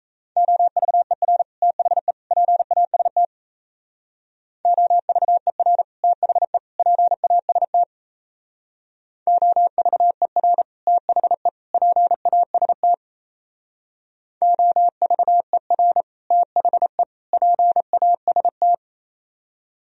Morse code, heard as a keyed tone, about 34 wpm, 700 Hz. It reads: OVER THE PAST OVER THE PAST OVER THE PAST OVER THE PAST
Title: Speed-Racing - Sets of 3 Words 28wpm